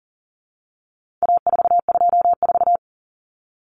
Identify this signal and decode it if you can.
Morse code, heard as a keyed tone, about 40 wpm, 700 Hz. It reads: A424
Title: Common Cut Numbers - 4-Digit 40wpm